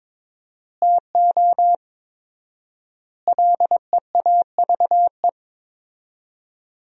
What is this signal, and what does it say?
Morse code, heard as a keyed tone, about 22 wpm, 700 Hz. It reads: TO LEAVE
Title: 4x Word-Spacing - Sentences from Top 400 Words - 22wpm